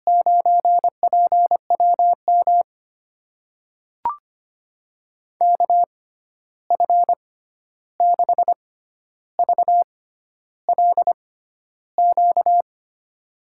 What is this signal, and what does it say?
Morse code, heard as a keyed tone, about 25 wpm, spaced as for 11 wpm, 700 Hz. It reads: 9PWM KF6VLQ
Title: USA Call Signs - 6x Character Spacing - 25wpm